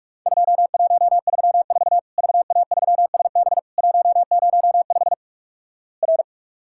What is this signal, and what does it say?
Morse code, heard as a keyed tone, about 47 wpm, 700 Hz. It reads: 2134 VA3SB 105 R
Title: Worked All Europe DX Contest - QTC Exchange - 45wpm